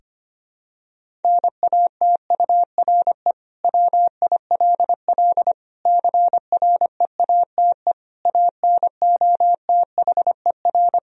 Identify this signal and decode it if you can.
Morse code, heard as a keyed tone, about 25 wpm, 700 Hz. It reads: NATURE WILL CREATE ANOTHER
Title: Sentences from Top 1000 Words - Review Part 3 - 25wpm